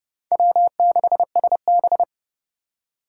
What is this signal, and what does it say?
Morse code, heard as a keyed tone, about 30 wpm, 700 Hz. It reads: W6SB
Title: USA Call Signs - 4 Characters - 4x Character Spacing - 30wpm